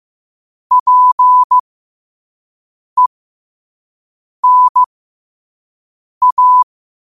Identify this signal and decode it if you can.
Morse code, heard as a keyed tone, about 15 wpm, spaced as for 5 wpm, 1000 Hz. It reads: PENA
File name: qotdfiles / 5wpm